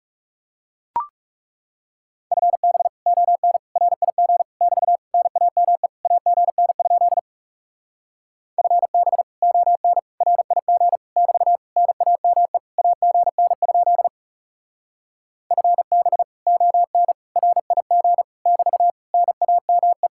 Morse code, heard as a keyed tone, about 37 wpm, 700 Hz. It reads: FB ON RIG <BT> NAME AGN? FB ON RIG <BT> NAME AGN? FB ON RIG <BT> NAME
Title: Speed-Racing Common QSO Phrases - 30wpm